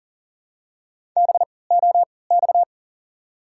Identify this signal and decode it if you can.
Morse code, heard as a keyed tone, about 40 wpm, spaced as for 23 wpm, 700 Hz. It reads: BOX
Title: 3x Character Spacing - Top 500 Words - 40wpm